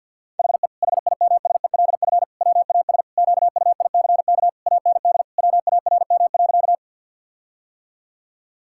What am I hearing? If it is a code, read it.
Morse code, heard as a keyed tone, about 50 wpm, 700 Hz. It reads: HE HIMSELF WAS QUICK AND WARM.
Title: Sentences from Top 1600 Words 50wpm